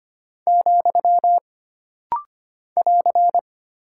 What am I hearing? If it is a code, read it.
Morse code, heard as a keyed tone, about 25 wpm, 700 Hz. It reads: , <AR>
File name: Single Punctuation-Prosign - Rapid-Fire - 25wpm